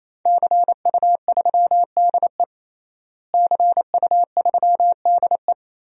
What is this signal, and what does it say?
Morse code, heard as a keyed tone, about 28 wpm, 700 Hz. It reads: CU3DE CU3DE
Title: DXCC-100 Call Signs - 5 Characters - Repeated 3x - 28wpm